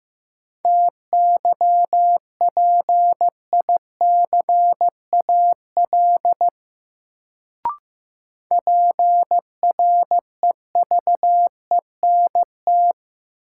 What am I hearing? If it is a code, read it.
Morse code, heard as a keyed tone, about 15 wpm, 700 Hz. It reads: TYPICAL PREVENT
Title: Sentences from Top 1100 Words - New Word Drill Alt - 15wpm